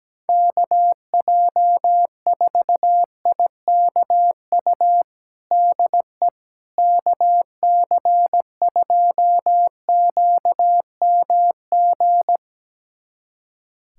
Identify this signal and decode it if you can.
Morse code, heard as a keyed tone, about 17 wpm, 700 Hz. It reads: KJ4IKU DE KC2QMG
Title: USA Callsign DE Callsign - 6 Characters - 17wpm